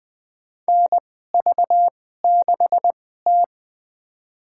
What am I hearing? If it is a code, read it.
Morse code, heard as a keyed tone, about 21 wpm, spaced as for 15 wpm, 700 Hz. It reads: NV6T
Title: USA Call Signs - 4 Characters - 2x Character Spacing - 20wpm